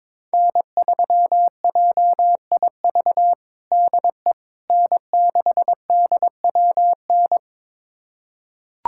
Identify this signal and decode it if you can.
Morse code, heard as a keyed tone, about 22 wpm, 700 Hz. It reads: N3JIV DE N6DWN
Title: USA Callsign DE Callsign - 5 Characters - 22wpm